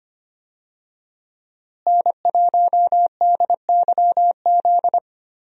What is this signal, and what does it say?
Morse code, heard as a keyed tone, about 25 wpm, 700 Hz. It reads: N1DYZ
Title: USA Call Signs - 5 Characters - Repeated 3x - 25wpm